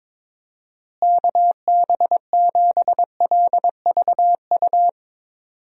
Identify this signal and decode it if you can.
Morse code, heard as a keyed tone, about 22 wpm, 700 Hz. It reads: KB7LVU
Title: USA Call Signs - 6 Characters - Repeated 3x - 22wpm